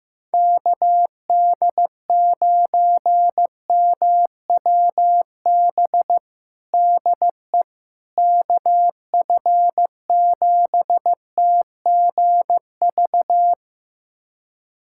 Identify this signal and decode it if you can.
Morse code, heard as a keyed tone, about 15 wpm, 700 Hz. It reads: KD9MWB DE KF7TGV
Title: USA Callsign DE Callsign - 6 Characters - 15wpm